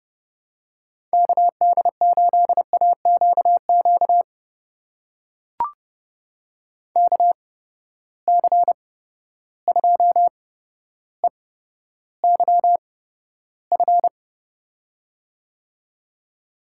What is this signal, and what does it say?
Morse code, heard as a keyed tone, about 32 wpm, spaced as for 13 wpm, 700 Hz. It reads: KD8AQQ KC2EYF
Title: USA Call Signs - 6 Characters - 8x Character Spacing - 30wpm